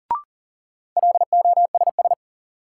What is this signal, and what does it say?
Morse code, heard as a keyed tone, about 40 wpm, 700 Hz. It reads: LOSS
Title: Sentences from Top 1400 Words - New Word Drill - 40wpm